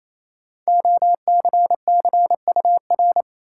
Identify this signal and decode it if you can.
Morse code, heard as a keyed tone, about 28 wpm, 700 Hz. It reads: OCCUR
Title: Sentences from Top 900 Words - New Word Drill - 28wpm